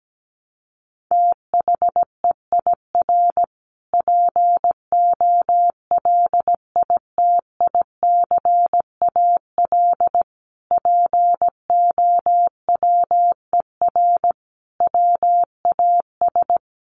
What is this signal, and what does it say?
Morse code, heard as a keyed tone, about 17 wpm, 700 Hz. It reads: THEIR POLITICAL POWER WAS
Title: Sentences from Top 1300 Words 17wpm